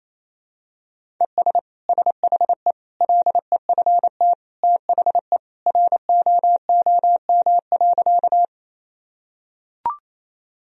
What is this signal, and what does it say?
Morse code, heard as a keyed tone, about 28 wpm, 700 Hz. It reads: ES SHE LEFT THE ROOM.
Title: Sentences from Top 300 Words and Abbreviations 28wpm